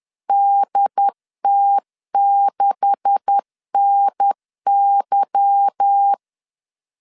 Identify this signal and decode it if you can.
Morse code, heard as a keyed tone, about 10 wpm, 800 Hz. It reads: DT6NY
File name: CW / DET / 1 9 1500C S B 10wpm 4439